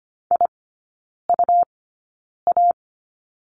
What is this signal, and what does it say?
Morse code, heard as a keyed tone, about 25 wpm, spaced as for 9 wpm, 700 Hz. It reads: IUA
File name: Lesson 17e - Letter u - Three Characters 25wpm